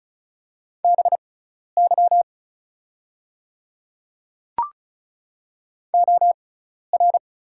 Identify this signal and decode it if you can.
Morse code, heard as a keyed tone, about 34 wpm, spaced as for 12 wpm, 700 Hz. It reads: BY OR
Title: 6x Character Spacing - Top 100 Words - 35wpm